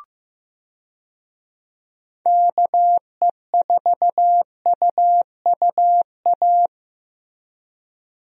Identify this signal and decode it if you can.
Morse code, heard as a keyed tone, about 15 wpm, 700 Hz. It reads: KE4UUA
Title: USA Call Signs - 6 Characters - Repeated 2x - 15wpm